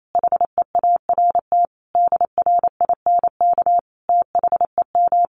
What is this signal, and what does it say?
Morse code, heard as a keyed tone, about 28 wpm, 700 Hz. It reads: HEART DRINK THEM
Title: Sentences from Top 1500 Words 28wpm